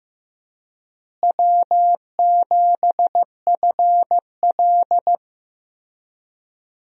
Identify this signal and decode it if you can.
Morse code, heard as a keyed tone, about 15 wpm, 700 Hz. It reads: W7FL
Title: USA Call Signs - 4 Characters - Repeated 2x - 15wpm